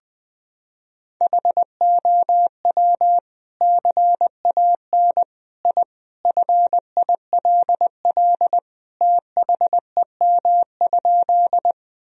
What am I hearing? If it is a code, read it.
Morse code, heard as a keyed tone, about 20 wpm, 700 Hz. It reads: HOW CAN I FILL THEM?
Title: Sentences from Top 1000 Words - Review Part 3 - 20wpm